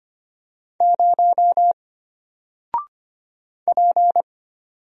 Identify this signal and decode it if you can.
Morse code, heard as a keyed tone, about 25 wpm, 700 Hz. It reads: 0 P
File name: Single-Letter-Number-25wpm